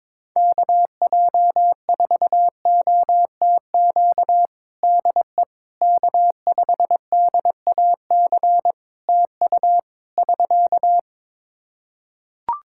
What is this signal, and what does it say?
Morse code, heard as a keyed tone, about 21 wpm, 700 Hz. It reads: KJ4OTQ DE K5DAC TU <SK>
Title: Practice QSO - Exchange 4 - Ending - 22wpm